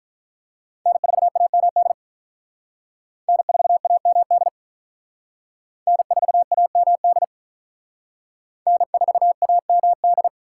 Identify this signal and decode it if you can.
Morse code, heard as a keyed tone, about 45 wpm, 700 Hz. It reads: N4AMD N4AMD N4AMD N4AMD